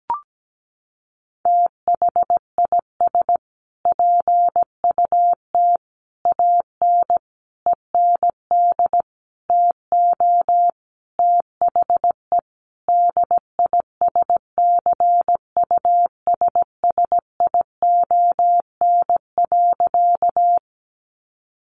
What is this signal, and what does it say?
Morse code, heard as a keyed tone, about 17 wpm, 700 Hz. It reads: THIS PUT AN END TO THE DISCUSSION.
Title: Sentences from Top 1100 Words 17wpm